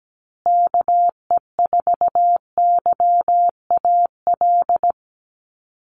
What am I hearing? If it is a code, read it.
Morse code, heard as a keyed tone, about 17 wpm, 700 Hz. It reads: KE4YAL
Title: USA Call Signs Repeated 3x 17wpm